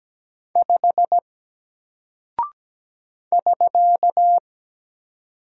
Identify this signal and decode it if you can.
Morse code, heard as a keyed tone, about 17 wpm, 700 Hz. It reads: 5 <SK>
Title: Single Character (Letter:Number:Punctuation:Prosign) Alt 17wpm